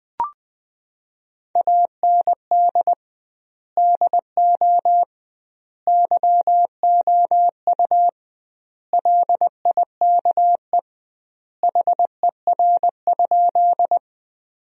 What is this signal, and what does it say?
Morse code, heard as a keyed tone, about 21 wpm, 700 Hz. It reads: AND DO YOU LIKE HER?
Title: 2x Word-Spacing - Sentences from Top 100 Words - 20wpm